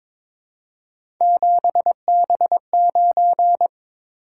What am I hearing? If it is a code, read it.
Morse code, heard as a keyed tone, about 22 wpm, 700 Hz. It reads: 7B9
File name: practice / 22wpm / Three Letter-Number Combinations - 22wpm